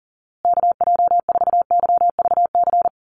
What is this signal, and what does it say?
Morse code, heard as a keyed tone, about 40 wpm, 700 Hz. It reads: KJ4YVC
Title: USA Call Signs Repeated 2x 40wpm